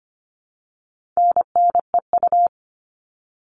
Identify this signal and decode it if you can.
Morse code, heard as a keyed tone, about 25 wpm, 700 Hz. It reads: NNEU
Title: All Cut Numbers - 4-Digit 25wpm